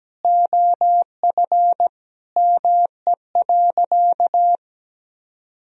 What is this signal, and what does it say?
Morse code, heard as a keyed tone, about 17 wpm, 700 Hz. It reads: OF ME.
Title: Sentences from Top 1000 Words - Review Part 4 - 17wpm